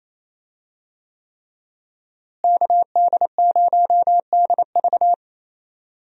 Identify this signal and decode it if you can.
Morse code, heard as a keyed tone, about 28 wpm, 700 Hz. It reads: KD0DV
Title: USA Call Signs - 5 Characters - Repeated 3x - 28wpm